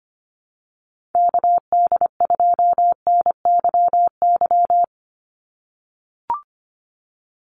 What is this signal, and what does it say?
Morse code, heard as a keyed tone, about 25 wpm, 700 Hz. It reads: KD2NYY
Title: USA Call Signs 25wpm